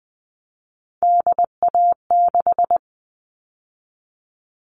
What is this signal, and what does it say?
Morse code, heard as a keyed tone, about 20 wpm, 700 Hz. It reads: DA6
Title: All Cut Numbers - 3-Digit 20wpm